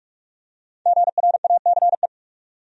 Morse code, heard as a keyed tone, about 44 wpm, 700 Hz. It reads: GRACE
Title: USA Common Names - Max 5 Letters - 45wpm